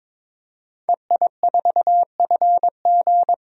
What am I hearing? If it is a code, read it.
Morse code, heard as a keyed tone, about 22 wpm, 700 Hz. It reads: EI4FG
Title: DXCC-100 Call Signs - 5 Characters - 22wpm